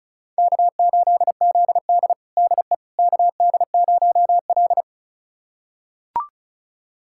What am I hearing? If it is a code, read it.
Morse code, heard as a keyed tone, about 36 wpm, 700 Hz. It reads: K8ZD DE KD0L
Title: USA Callsign DE Callsign - 4 Characters - 35wpm